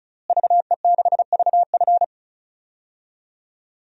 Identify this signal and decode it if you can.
Morse code, heard as a keyed tone, about 36 wpm, 700 Hz. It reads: VE6VF
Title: DXCC-100 Call Signs - 5 Characters - Repeated 2x - 35wpm